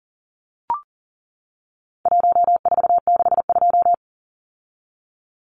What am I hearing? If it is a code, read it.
Morse code, heard as a keyed tone, about 40 wpm, 700 Hz. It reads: 1462
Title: Numbers - 4-Digit 40wpm